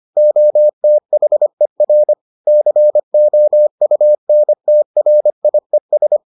Morse code, heard as a keyed tone, about 25 wpm, 600 Hz. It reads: OTHER COUNTRIES